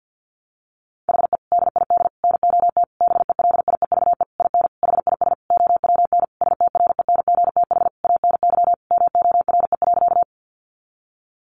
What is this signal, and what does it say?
Morse code, heard as a keyed tone, about 50 wpm, 700 Hz. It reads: HE DID NOT BELIEVE IN HIS OWN STRENGTH ANY MORE.
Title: Sentences from Top 1300 Words 50wpm